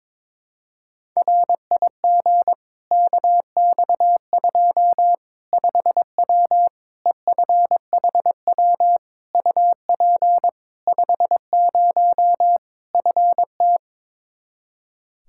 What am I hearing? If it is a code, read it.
Morse code, heard as a keyed tone, about 22 wpm, 700 Hz. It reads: RIG KX2 5W EFHW UP 50 FT
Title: Practice QSO - Exchange 2 - Info - 22wpm